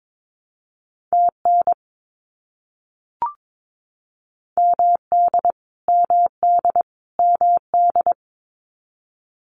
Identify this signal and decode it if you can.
Morse code, heard as a keyed tone, about 22 wpm, 700 Hz. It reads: TN MD MD MD
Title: US States Repeated 3x 22wpm